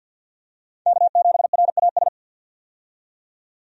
Morse code, heard as a keyed tone, about 51 wpm, 700 Hz. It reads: K7RAS
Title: USA Call Signs - 5 Characters - 50wpm